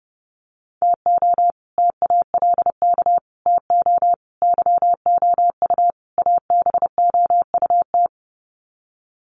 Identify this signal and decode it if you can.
Morse code, heard as a keyed tone, about 30 wpm, 700 Hz. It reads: TO TALK TO YOU ABOUT
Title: Sets of 5 Words Encore 30wpm